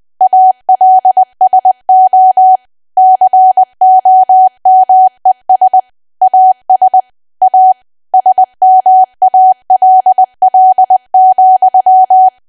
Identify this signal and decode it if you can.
Morse code, heard as a keyed tone, about 20 wpm, 750 Hz. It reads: ALSO COMES AS A SMALL,
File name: cwa / W1AW-Level3 / 20WPM